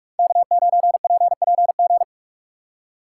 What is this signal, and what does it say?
Morse code, heard as a keyed tone, about 44 wpm, 700 Hz. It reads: K9PPG
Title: USA Call Signs - 5 Characters - 3x Character Spacing - 45wpm